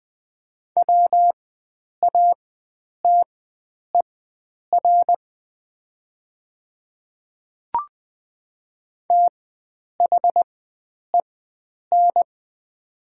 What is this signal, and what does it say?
Morse code, heard as a keyed tone, about 20 wpm, spaced as for 9 wpm, 700 Hz. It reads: WATER THEN
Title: 4x Character Spacing - Top 100 Words - 20wpm